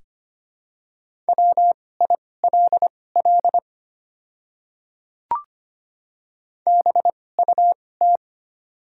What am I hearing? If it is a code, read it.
Morse code, heard as a keyed tone, about 25 wpm, spaced as for 18 wpm, 700 Hz. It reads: WILL BUT
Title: 2x Character Spacing - Top 200 Words - 25wpm